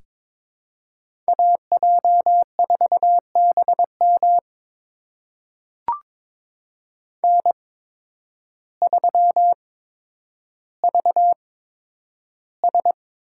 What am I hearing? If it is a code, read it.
Morse code, heard as a keyed tone, about 22 wpm, spaced as for 10 wpm, 700 Hz. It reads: AJ4BM N3VS
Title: USA Call Signs - 8x Character Spacing - 22wpm